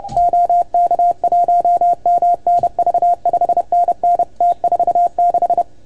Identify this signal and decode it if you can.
Morse code, heard as a keyed tone, about 29 wpm, 690 Hz. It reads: OK1MNV 5NN T46
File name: Archiv / CUC310 / OK1FGD CUC 310 pileup